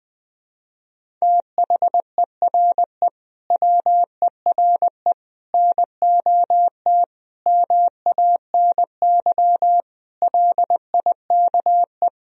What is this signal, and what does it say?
Morse code, heard as a keyed tone, about 20 wpm, 700 Hz. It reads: THERE WERE NOT MANY LIKE
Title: Sentences from Top 1000 Words - Review Part 1 - 20wpm